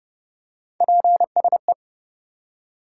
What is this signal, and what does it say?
Morse code, heard as a keyed tone, about 30 wpm, 700 Hz. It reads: PSE
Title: Top 90 QSO Elements 30wpm